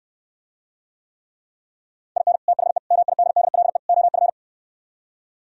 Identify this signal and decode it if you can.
Morse code, heard as a keyed tone, about 67 wpm, 700 Hz. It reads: IN THE MIDDLE OF